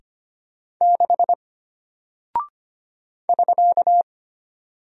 Morse code, heard as a keyed tone, about 25 wpm, 700 Hz. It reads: 6 <SK>
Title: Single Character (Letter:Number:Punctuation:Prosign) Alt 25wpm